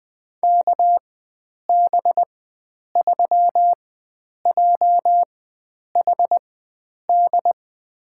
Morse code, heard as a keyed tone, about 20 wpm, spaced as for 9 wpm, 700 Hz. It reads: KB3JHD
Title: USA Call Signs - 6 Characters - 4x Character Spacing - 20wpm